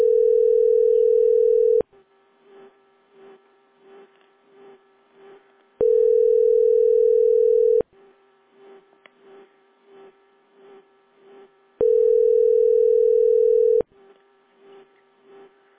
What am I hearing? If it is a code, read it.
Ringback tone (North America), 440 + 480 Hz.
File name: 9c -87e5-4af9-b1cd